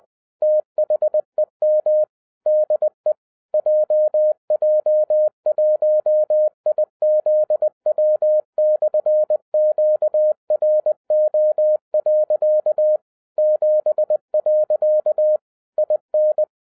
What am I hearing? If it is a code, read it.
Morse code, heard as a keyed tone, about 20 wpm, 610 Hz. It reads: THEM DE JJ1IZW/QRO. 7. IN
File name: 2011-10-mp3 / n13-TIPS FROM MY 34 YEARS OF EXPERIENCE 20wpm